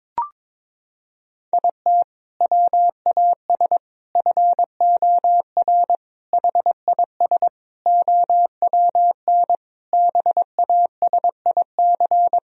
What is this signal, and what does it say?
Morse code, heard as a keyed tone, about 22 wpm, 700 Hz. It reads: IT WAS FOR HIS OWN BASIC
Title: Sentences from Top 1000 Words - Review Part 3 - 22wpm